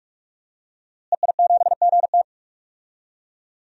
Morse code, heard as a keyed tone, about 46 wpm, 700 Hz. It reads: EI7GT
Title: DXCC-100 Call Signs - 5 Characters - Repeated 3x - 45wpm